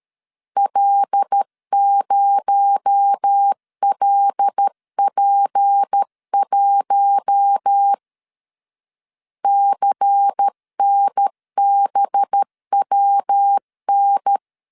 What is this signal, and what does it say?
Morse code, heard as a keyed tone, about 13 wpm, 800 Hz. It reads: L0LP1 CNBWN
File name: CW / DET / 1 6 1500C S C 12wpm 3533